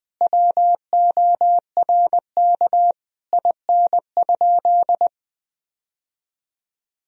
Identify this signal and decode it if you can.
Morse code, heard as a keyed tone, about 20 wpm, 700 Hz. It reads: WORK IN?
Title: Sentences from Top 1000 Words - Review Part 4 - 20wpm